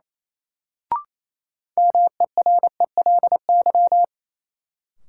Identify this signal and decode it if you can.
Morse code, heard as a keyed tone, about 28 wpm, 700 Hz. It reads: MERELY
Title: Sentences from Top 1400 Words - New Word Drill Alt - 28wpm